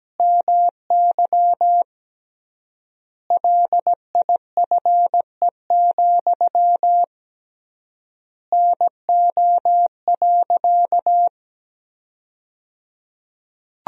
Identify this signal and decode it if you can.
Morse code, heard as a keyed tone, about 17 wpm, 700 Hz. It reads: MY LIFE, NO.
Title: 3x Word-Spacing - Sentences from Top 300 Words - 17wpm